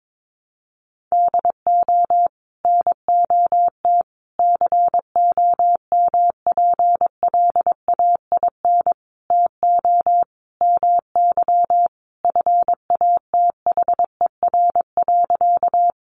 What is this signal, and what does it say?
Morse code, heard as a keyed tone, about 22 wpm, 700 Hz. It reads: DO NOT COMPLAIN TO MY FATHER.
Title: Sentences from Top 1300 Words 22wpm